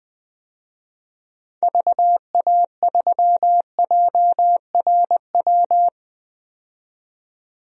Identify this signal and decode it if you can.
Morse code, heard as a keyed tone, about 20 wpm, 700 Hz. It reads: VA3JRW
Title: DXCC-100 Call Signs - 6 Characters - 20wpm